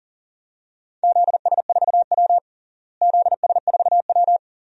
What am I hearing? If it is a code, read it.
Morse code, heard as a keyed tone, about 40 wpm, 700 Hz. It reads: ZS4W ZS4W
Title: DXCC-100 Call Signs - 4 Characters - Repeated 2x - 40wpm